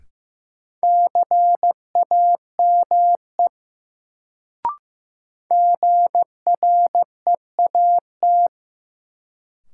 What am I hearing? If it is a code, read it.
Morse code, heard as a keyed tone, about 15 wpm, 700 Hz. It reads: CAME GREAT
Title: Sentences from Top 200 Words - New Word Drill - 15wpm